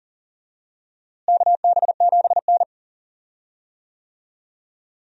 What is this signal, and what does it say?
Morse code, heard as a keyed tone, about 40 wpm, 700 Hz. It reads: KB7N
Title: USA Call Signs - 4 Characters - Repeated 3x - 40wpm